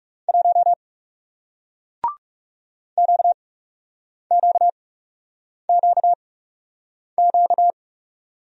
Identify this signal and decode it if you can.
Morse code, heard as a keyed tone, about 37 wpm, spaced as for 8 wpm, 700 Hz. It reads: 1 QQQQ
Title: Speed-Racing Single Letter-Number 30wpm